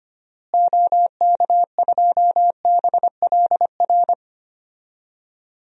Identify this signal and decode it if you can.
Morse code, heard as a keyed tone, about 25 wpm, 700 Hz. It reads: OK2BLR
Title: DXCC-100 Call Signs - 6 Characters - Repeated 3x - 25wpm